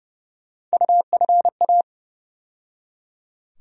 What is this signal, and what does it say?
Morse code, heard as a keyed tone, about 30 wpm, 700 Hz. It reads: UFA
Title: USA Call Sign Postfixes - 3 Letters 30wpm